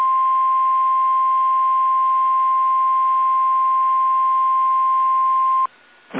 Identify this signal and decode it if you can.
Emergency Alert System (SAME) tones (the 1050 Hz attention tone). It is heard from 1.1 to 5.5 s.
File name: eas / KSDZKDJL / KSDZKDJL